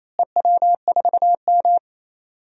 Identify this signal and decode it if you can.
Morse code, heard as a keyed tone, about 27 wpm, 700 Hz. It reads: EW4M
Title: DXCC-100 Call Signs - 4 Characters - Repeated 3x - 28wpm